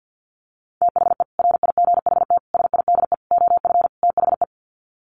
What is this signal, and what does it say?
Morse code, heard as a keyed tone, about 50 wpm, 700 Hz. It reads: THE RIGHT SIDE OF THE
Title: Sets of 5 Words Encore 50wpm